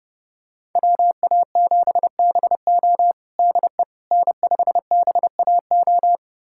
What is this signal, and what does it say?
Morse code, heard as a keyed tone, about 30 wpm, 700 Hz. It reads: WA7BO DE N5BAO
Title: USA Callsign DE Callsign - 5 Characters - 30wpm